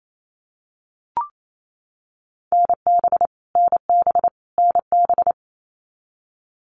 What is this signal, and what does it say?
Morse code, heard as a keyed tone, about 28 wpm, 700 Hz. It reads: NB NB NB
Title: Canadian Provinces and Territories - Repeated 3x - 28wpm